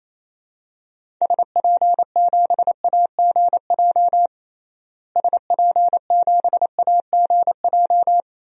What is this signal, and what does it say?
Morse code, heard as a keyed tone, about 28 wpm, 700 Hz. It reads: SP7AGJ SP7AGJ
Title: DXCC-100 Call Signs - 6 Characters - Repeated 2x - 28wpm